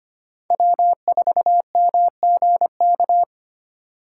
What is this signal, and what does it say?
Morse code, heard as a keyed tone, about 33 wpm, 700 Hz. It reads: W4MGK